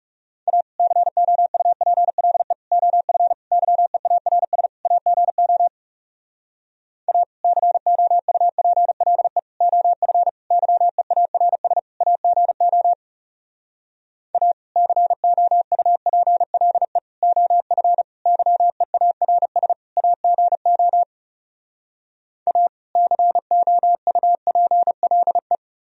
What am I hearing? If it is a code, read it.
Morse code, heard as a keyed tone, about 40 wpm, 700 Hz. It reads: A COUPLE OF YEARS AGO A COUPLE OF YEARS AGO A COUPLE OF YEARS AGO A COUPLE